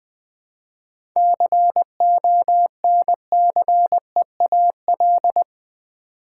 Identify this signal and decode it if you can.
Morse code, heard as a keyed tone, about 20 wpm, 700 Hz. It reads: CONCEAL
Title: 7 Letter Words - First 3 Letters Given - 20wpm